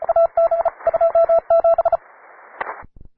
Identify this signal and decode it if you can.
Morse code, heard as a keyed tone, about 34 wpm, 670 Hz, with radio noise. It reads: AB 27